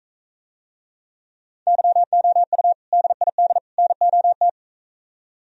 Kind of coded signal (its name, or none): Morse code